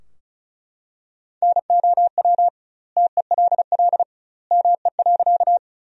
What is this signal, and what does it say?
Morse code, heard as a keyed tone, about 36 wpm, 700 Hz. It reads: NOW TELL ME.
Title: 2x Word-Spacing - Sentences from Top 200 Words - 35wpm